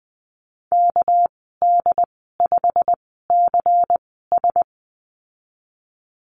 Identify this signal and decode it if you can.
Morse code, heard as a keyed tone, about 20 wpm, spaced as for 14 wpm, 700 Hz. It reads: KD5CS
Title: 213 Lesson 34g - Number 8 - Call Signs 20wpm